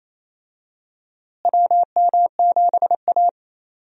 Morse code, heard as a keyed tone, about 28 wpm, 700 Hz. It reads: WM7A
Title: USA Call Signs - 4 Characters - Repeated 3x - 28wpm